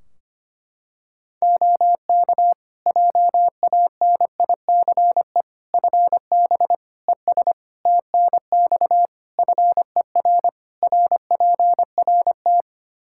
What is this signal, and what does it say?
Morse code, heard as a keyed tone, about 25 wpm, 700 Hz. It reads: OK JANICE FB ES TNX FER RPRT
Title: K1USN SST QSO Practice 25wpm